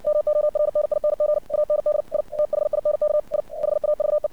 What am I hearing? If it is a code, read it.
Morse code, heard as a keyed tone, about 60 wpm, 600 Hz. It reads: MORNING AND I THINK I HAVE
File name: archived / CRSnet